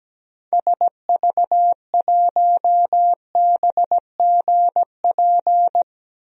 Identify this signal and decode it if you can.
Morse code, heard as a keyed tone, about 17 wpm, 700 Hz. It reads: SV1BGP
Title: DXCC-100 Call Signs - 6 Characters - 17wpm